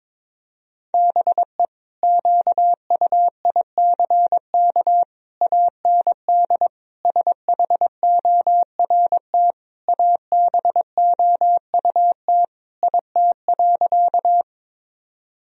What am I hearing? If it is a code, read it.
Morse code, heard as a keyed tone, about 22 wpm, 700 Hz. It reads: BE QUICK AND SHORT ABOUT IT.
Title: Sentences from Top 1000 Words - Review Part 2 - 22wpm